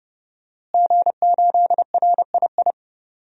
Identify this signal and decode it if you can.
Morse code, heard as a keyed tone, about 30 wpm, 700 Hz. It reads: G8RII
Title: DXCC-100 Call Signs - 5 Characters - Repeated 3x - 30wpm